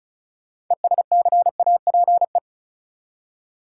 Morse code, heard as a keyed tone, about 36 wpm, 700 Hz. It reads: ESCAPE
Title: Six Letter Words - 35wpm